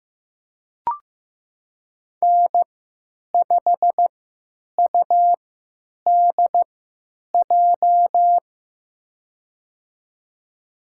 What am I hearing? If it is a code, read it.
Morse code, heard as a keyed tone, about 15 wpm, spaced as for 9 wpm, 700 Hz. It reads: N5UDJ
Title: USA Call Signs - 5 Characters - 3x Character Spacing - 15wpm